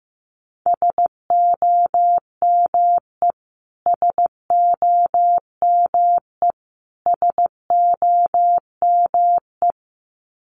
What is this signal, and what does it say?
Morse code, heard as a keyed tone, about 15 wpm, 700 Hz. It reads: SOME SOME SOME
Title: Top 500 Words Repeated 15wpm